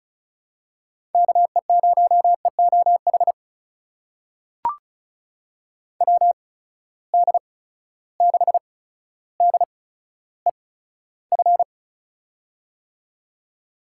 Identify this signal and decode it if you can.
Morse code, heard as a keyed tone, about 36 wpm, spaced as for 15 wpm, 700 Hz. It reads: KE0EOH WD6DEF
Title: USA Call Signs - 6 Characters - 8x Character Spacing - 35wpm